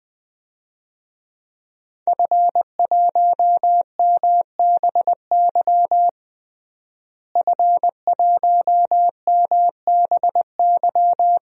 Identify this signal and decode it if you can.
Morse code, heard as a keyed tone, about 20 wpm, 700 Hz. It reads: F1MBY F1MBY
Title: DXCC-100 Call Signs - 5 Characters - Repeated 3x - 20wpm